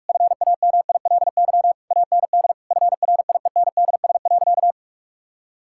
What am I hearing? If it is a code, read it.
Morse code, heard as a keyed tone, about 42 wpm, 700 Hz. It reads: FAMILY AND FRIENDS.
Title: Sentences from Top 1000 Words - Review Part 5 - 45wpm